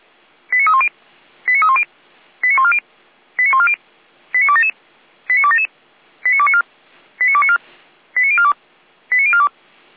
Five-tone selective call (ZVEI), heard as quick runs of five tones, tones of about 70 ms. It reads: 89310 89310 89140 89140 89277 89277 89294 89294 89042 89042